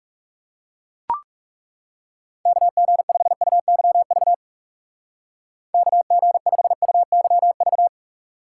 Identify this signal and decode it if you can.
Morse code, heard as a keyed tone, about 37 wpm, 700 Hz. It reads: KG5UYV KG5UYV
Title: Speed Racing - USA 2x3 Call Signs 30wpm